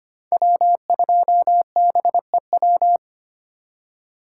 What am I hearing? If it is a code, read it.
Morse code, heard as a keyed tone, about 25 wpm, 700 Hz. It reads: W2BEW
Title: USA Call Signs - 5 Characters - 3x Character Spacing - 25wpm